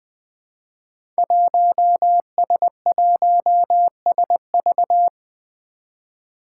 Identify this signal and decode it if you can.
Morse code, heard as a keyed tone, about 20 wpm, 700 Hz. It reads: 1S1SV
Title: Five Challenging Letter-Number Combinations 20wpm